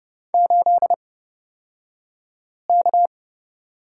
Morse code, heard as a keyed tone, about 29 wpm, spaced as for 5 wpm, 690 Hz. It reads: 8K
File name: Lesson 36d - BT - Two Characters 30wpm